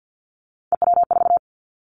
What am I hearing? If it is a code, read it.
Morse code, heard as a keyed tone, about 50 wpm, 700 Hz. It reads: EW4
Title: Callsign Prefixes for Top 50 DXCC - 50wpm